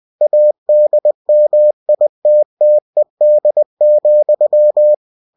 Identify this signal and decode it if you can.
Morse code, heard as a keyed tone, about 20 wpm, 600 Hz. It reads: ADMITTED,
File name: cw-books / winnie-the-pooh / winnie-20wpm-page-0004